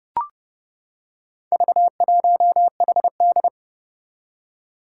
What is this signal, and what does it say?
Morse code, heard as a keyed tone, about 30 wpm, 700 Hz. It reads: V1HD
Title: Four Challenging Letter-Number Combinations 30wpm